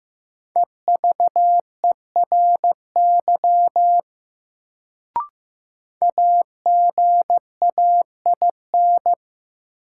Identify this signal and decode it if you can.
Morse code, heard as a keyed tone, about 15 wpm, 700 Hz. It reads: EVERY AGAIN
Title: Sentences from Top 200 Words - New Word Drill - 15wpm